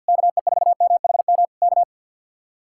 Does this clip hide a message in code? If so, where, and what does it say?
Yes: Morse code from 0.1 to 1.8 s, heard as a keyed tone, about 65 wpm, 700 Hz. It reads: KE4MHM K